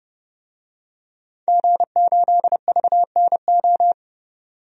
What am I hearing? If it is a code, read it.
Morse code, heard as a keyed tone, about 32 wpm, 700 Hz. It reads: G8VNO
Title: DXCC-100 Call Signs - 5 Characters - 30wpm